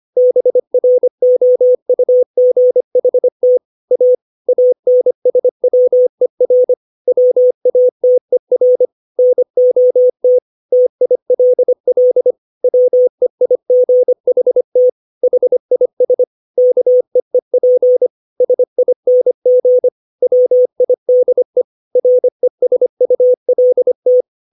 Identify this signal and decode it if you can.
Morse code, heard as a keyed tone, about 25 wpm, 500 Hz. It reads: BROUGHT A ANSWER WATER NOT TILL WEIGHT HIS KEEP SING WIDE RESULT
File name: 632 words / 632 Words 25WPM